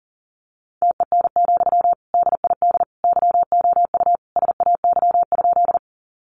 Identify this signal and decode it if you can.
Morse code, heard as a keyed tone, about 40 wpm, 700 Hz. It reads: TEN, DID YOU SAY?
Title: Sentences from Top 400 Words 40wpm